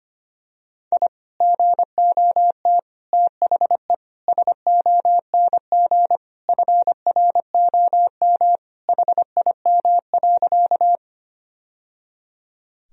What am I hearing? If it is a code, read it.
Morse code, heard as a keyed tone, about 25 wpm, 700 Hz. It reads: I GOT THE SONG FROM HIM.
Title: Sentences from Top 1000 Words - Review Part 2 - 25wpm